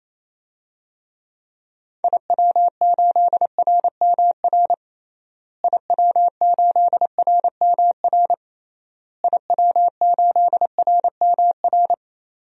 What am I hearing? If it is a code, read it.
Morse code, heard as a keyed tone, about 28 wpm, 700 Hz. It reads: IW8RMR IW8RMR IW8RMR
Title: DXCC-100 Call Signs - 6 Characters - Repeated 3x - 28wpm